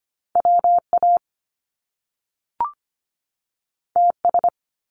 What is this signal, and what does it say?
Morse code, heard as a keyed tone, about 25 wpm, 700 Hz. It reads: WA TS
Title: Top Two Letter Combinations in Words 25wpm